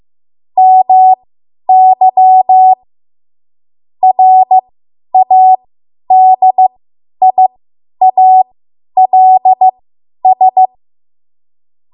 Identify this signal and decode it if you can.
Morse code, heard as a keyed tone, about 15 wpm, spaced as for 10 wpm, 750 Hz. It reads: MY RADIALS
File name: Archive / 10 WPM / 10WPM